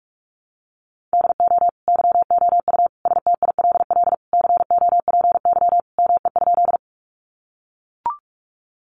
Morse code, heard as a keyed tone, about 44 wpm, 700 Hz. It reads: DO YOU STILL COPY ME?
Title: Sentences from Top 1400 Words 45wpm